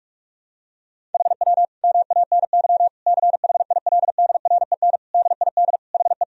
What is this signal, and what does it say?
Morse code, heard as a keyed tone, about 48 wpm, 700 Hz. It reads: HW MANY CHILDREN DID HE
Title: Sentences from Top 400 Words and Abbreviations 45wpm